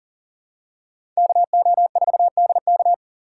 Morse code, heard as a keyed tone, about 40 wpm, 700 Hz. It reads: KO4DK
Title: USA Call Signs - 5 Characters - 3x Character Spacing - 40wpm